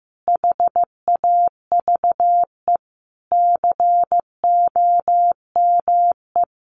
Morse code, heard as a keyed tone, about 15 wpm, 700 Hz. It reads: HAVE COME
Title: Sentences from Top 1000 Words 15wpm